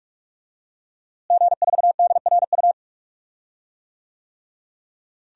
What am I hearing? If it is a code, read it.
Morse code, heard as a keyed tone, about 44 wpm, 700 Hz. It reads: G4DRU
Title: DXCC-100 Call Signs - 5 Characters - Repeated 3x - 45wpm